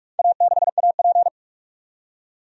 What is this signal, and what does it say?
Morse code, heard as a keyed tone, about 42 wpm, 700 Hz. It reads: A6AP
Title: DXCC-100 Call Signs - 4 Characters - 45wpm